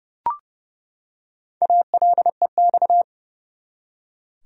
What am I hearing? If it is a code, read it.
Morse code, heard as a keyed tone, about 30 wpm, 700 Hz. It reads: ALEX
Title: USA Common Names - Max 4 Letters - 30wpm